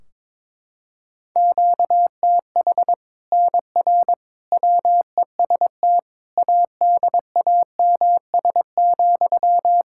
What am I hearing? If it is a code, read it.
Morse code, heard as a keyed tone, about 21 wpm, 700 Hz. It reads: QTH NR WEST ADAMS,
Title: Practice QSO - Exchange 1 - Beginning - 22wpm